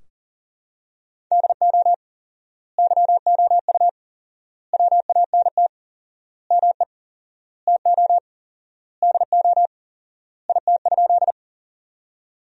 Morse code, heard as a keyed tone, about 40 wpm, 700 Hz. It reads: DO YOU WANT ME TO DO IT?
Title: 4x Word-Spacing - Sentences from Top 200 Words - 40wpm